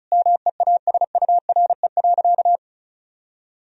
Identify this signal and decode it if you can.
Morse code, heard as a keyed tone, about 34 wpm, 700 Hz. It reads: MEASURE.
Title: Sentences from Top 1000 Words - Review Part 1 - 35wpm